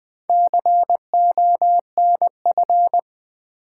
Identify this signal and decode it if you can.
Morse code, heard as a keyed tone, about 20 wpm, 700 Hz. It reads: CONF
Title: Build A Word - 4 Letters At A Time - 8-Letter Words - 20wpm